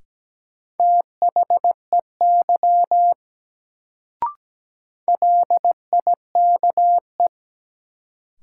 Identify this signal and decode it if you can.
Morse code, heard as a keyed tone, about 17 wpm, 700 Hz. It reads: THEY LIKE
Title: Sentences from Top 100 Words - New Word Drill - 17wpm